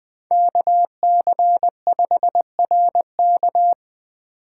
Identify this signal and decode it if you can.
Morse code, heard as a keyed tone, about 20 wpm, 700 Hz. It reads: KC5RK
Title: USA Call Signs - 4x Character Spacing - 20wpm